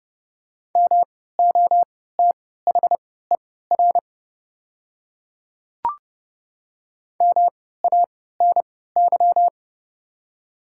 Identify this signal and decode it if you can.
Morse code, heard as a keyed tone, about 30 wpm, spaced as for 17 wpm, 700 Hz. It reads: MOTHER MANY
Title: 3x Character Spacing - Top 500 Words - 30wpm